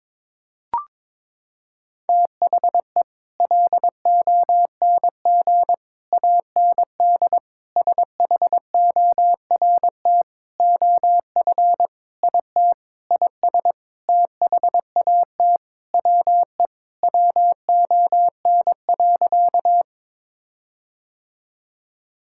Morse code, heard as a keyed tone, about 22 wpm, 700 Hz. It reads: THE LONG AND SHORT OF IT IS THAT WE WON.
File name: practice / 22wpm / Sentences with 8-Word Phrases - 22wpm